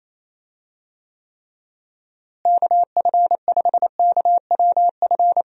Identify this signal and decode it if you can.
Morse code, heard as a keyed tone, about 28 wpm, 700 Hz. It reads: KF5KWF
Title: USA Call Signs - 6 Characters - Repeated 2x - 28wpm